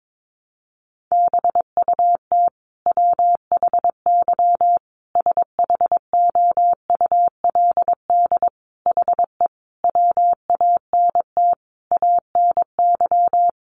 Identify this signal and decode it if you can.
Morse code, heard as a keyed tone, about 22 wpm, 700 Hz. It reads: BUT WHY SHOULD HE WANT ANY
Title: Sentences from Top 1200 Words 22wpm